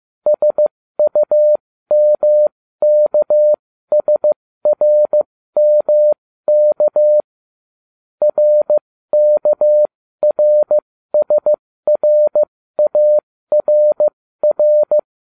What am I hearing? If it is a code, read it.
Morse code, heard as a keyed tone, about 15 wpm, spaced as for 13 wpm, 600 Hz. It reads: SUMKSRMK RKRSRARR